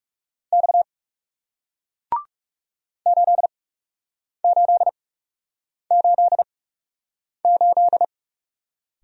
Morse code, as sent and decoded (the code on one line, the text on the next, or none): -..- / ---.. ---.. ---.. ---..
X 8888